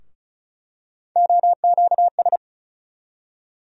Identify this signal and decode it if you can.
Morse code, heard as a keyed tone, about 36 wpm, 700 Hz. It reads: OQS
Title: USA Call Sign Postfixes - 3 Letters 35wpm